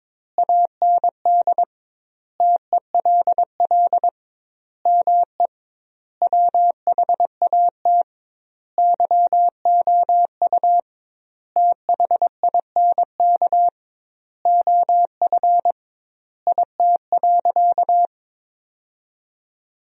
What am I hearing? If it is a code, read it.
Morse code, heard as a keyed tone, about 22 wpm, 700 Hz. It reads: AND TELL ME WHAT YOU THINK OF IT.
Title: 2x Word-Spacing - Sentences from Top 200 Words - 22wpm